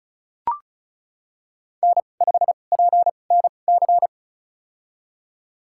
Morse code, heard as a keyed tone, about 34 wpm, spaced as for 25 wpm, 700 Hz. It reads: N5PNC
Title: USA Call Signs - 5 Characters - 2x Character Spacing - 35wpm